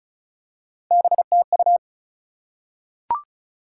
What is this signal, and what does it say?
Morse code, heard as a keyed tone, about 36 wpm, 700 Hz. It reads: BTU
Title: Top 180 QSO Elements 35wpm